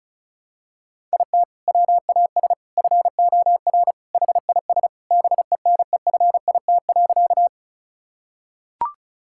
Morse code, heard as a keyed tone, about 36 wpm, 700 Hz. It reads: IT WAS FOR HIS BENEFIT.
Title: Sentences from Top 1000 Words - Review Part 6 - 35wpm